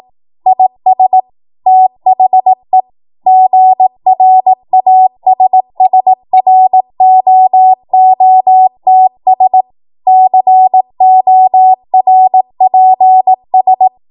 Morse code, heard as a keyed tone, about 18 wpm, 750 Hz. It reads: IS THE GRASSROOTS CORPS
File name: Morse / 18WPM